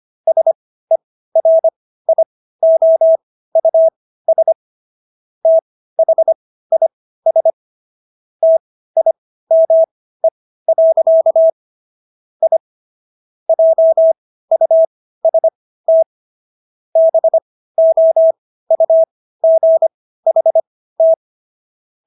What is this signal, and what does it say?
Morse code, heard as a keyed tone, about 25 wpm, spaced as for 15 wpm, 650 Hz. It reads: SERIOUS THIS TIME. I JUST BOUGHT